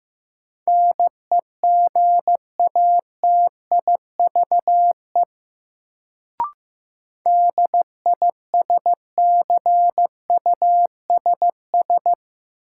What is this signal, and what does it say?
Morse code, heard as a keyed tone, about 15 wpm, 700 Hz. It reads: NEGATIVE DISCUSS
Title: Sentences from Top 1000 Words - New Word Drill - 15wpm